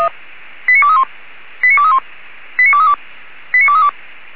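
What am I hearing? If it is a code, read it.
Five-tone selective call (ZVEI), heard as quick runs of five tones, tones of about 80 ms. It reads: 89231 89231 89232 89232
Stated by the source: Received by SDR.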